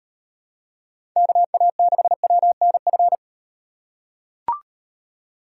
Morse code, heard as a keyed tone, about 38 wpm, 700 Hz. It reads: KA6WNF